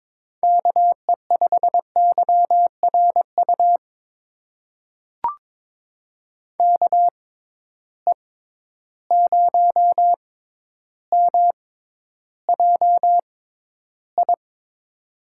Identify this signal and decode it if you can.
Morse code, heard as a keyed tone, about 22 wpm, spaced as for 11 wpm, 700 Hz. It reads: KE5YRU KE0MJI
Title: USA Call Signs - 6 Characters - 6x Character Spacing - 22wpm